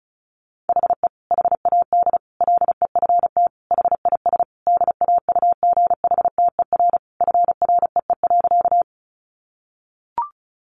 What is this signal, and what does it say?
Morse code, heard as a keyed tone, about 34 wpm, 700 Hz. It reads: HE HAD LEFT HIS DAUGHTER FREE.
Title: Sentences from Top 1500 Words 35wpm